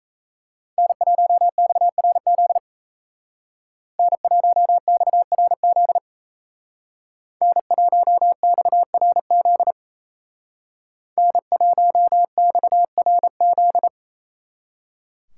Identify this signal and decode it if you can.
Morse code, heard as a keyed tone, about 37 wpm, 700 Hz. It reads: N1XRZ N1XRZ N1XRZ N1XRZ